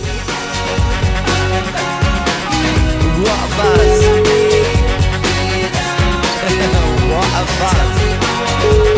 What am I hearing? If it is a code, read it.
Ringback tone (Europe, 425 Hz), with noise.